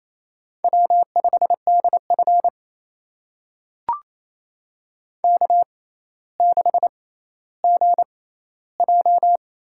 Morse code, heard as a keyed tone, about 28 wpm, spaced as for 14 wpm, 700 Hz. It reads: W5DF K6GJ
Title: USA Call Signs - 4 Characters - 6x Character Spacing - 28wpm